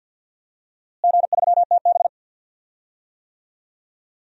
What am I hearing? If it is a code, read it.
Morse code, heard as a keyed tone, about 52 wpm, 700 Hz. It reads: G3TB
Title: DXCC-100 Call Signs - 4 Characters - Repeated 2x - 50wpm